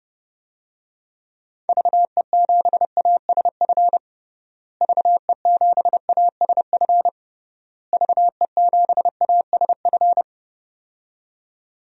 Morse code, heard as a keyed tone, about 32 wpm, 700 Hz. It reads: VE7ASF VE7ASF VE7ASF
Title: DXCC-100 Call Signs - 6 Characters - Repeated 3x - 30wpm